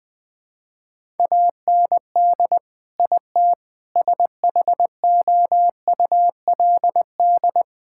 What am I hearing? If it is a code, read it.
Morse code, heard as a keyed tone, about 20 wpm, 700 Hz. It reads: AND IT SHOULD
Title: Sentences from Top 1000 Words - Review Part 6 - 20wpm